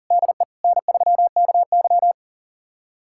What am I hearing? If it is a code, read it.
Morse code, heard as a keyed tone, about 40 wpm, 700 Hz. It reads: DE N3KY
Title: USA Callsign DE Callsign - 4 Characters - 40wpm